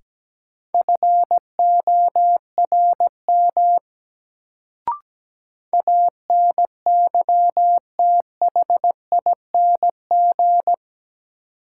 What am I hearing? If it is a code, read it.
Morse code, heard as a keyed tone, about 17 wpm, 700 Hz. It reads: FORM ANYTHING
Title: Sets of 4 Words - New Word Drill - 17wpm